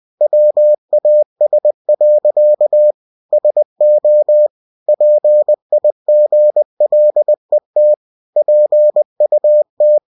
Morse code, heard as a keyed tone, about 20 wpm, 600 Hz. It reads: WAS. SO PIGLET PUT
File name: cw-books / winnie-the-pooh / winnie-20wpm-page-0038